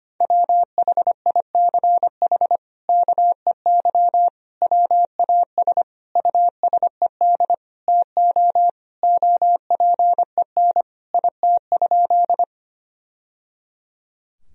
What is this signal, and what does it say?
Morse code, heard as a keyed tone, about 25 wpm, 700 Hz. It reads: WHICH KEY WAS USED TO OPEN IT?
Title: Sentences from Top 1000 Words - Review Part 4 - 25wpm